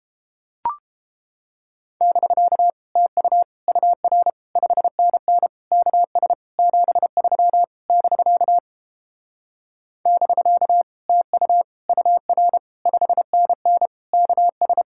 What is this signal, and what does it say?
Morse code, heard as a keyed tone, about 29 wpm, 700 Hz. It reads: <BK> TU UR 5NN KS 73 <BK> <BK> TU UR 5NN KS